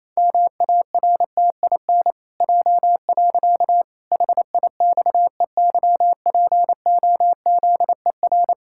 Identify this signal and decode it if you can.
Morse code, heard as a keyed tone, about 28 wpm, 700 Hz. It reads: MARTIN J. HIXEYPOZER
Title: Pangram Challenge v2 - 28wpm